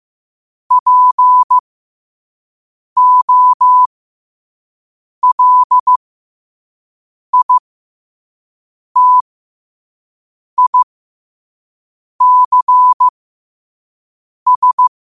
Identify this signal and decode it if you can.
Morse code, heard as a keyed tone, about 15 wpm, spaced as for 5 wpm, 1000 Hz. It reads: POLITICS
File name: 2026-02-15-5wpm